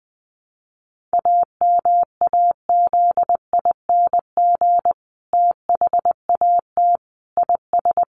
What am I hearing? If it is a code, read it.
Morse code, heard as a keyed tone, about 20 wpm, 700 Hz. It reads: AMAZING THAT IS
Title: Sentences from Top 1500 Words 20wpm